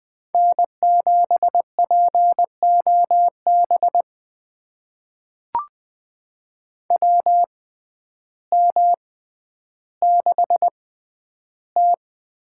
Morse code, heard as a keyed tone, about 20 wpm, spaced as for 11 wpm, 700 Hz. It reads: N7POB WM6T
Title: USA Call Signs - 6x Character Spacing - 20wpm